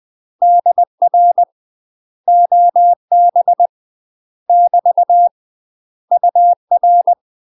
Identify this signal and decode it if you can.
Morse code, heard as a keyed tone, about 20 wpm, 700 Hz. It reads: DR OB <BT> UR
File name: UK MASTER QSOs TRAINING / 20WPM / UK MASTER QSOs 0028